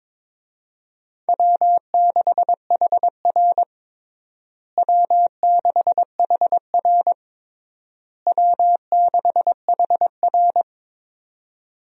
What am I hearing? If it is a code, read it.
Morse code, heard as a keyed tone, about 21 wpm, 700 Hz. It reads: W6HR W6HR W6HR
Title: USA Call Signs - 4 Characters - Repeated 3x - 22wpm